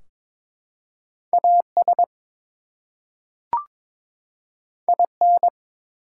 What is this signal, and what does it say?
Morse code, heard as a keyed tone, about 22 wpm, 700 Hz. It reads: AS IN
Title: Two Letter Words - 22wpm